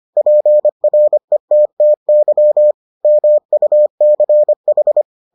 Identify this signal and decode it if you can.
Morse code, heard as a keyed tone, about 25 wpm, 600 Hz. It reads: PRETTY MUCH